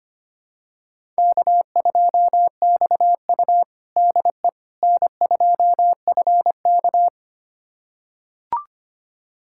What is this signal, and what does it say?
Morse code, heard as a keyed tone, about 25 wpm, 700 Hz. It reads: K2XU DE N2FK
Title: USA Callsign DE Callsign - 4 Characters - 25wpm